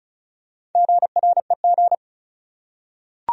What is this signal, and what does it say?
Morse code, heard as a keyed tone, about 35 wpm, 700 Hz. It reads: GREG
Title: USA Common Names - Max 5 Letters - 35wpm